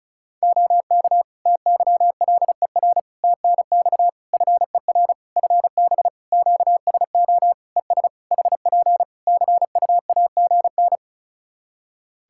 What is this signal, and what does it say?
Morse code, heard as a keyed tone, about 36 wpm, 700 Hz. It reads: OK TYLER TNX FER FB QSO ES HP CUAGN
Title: Practice QSO - Exchange 4 - Ending - 35wpm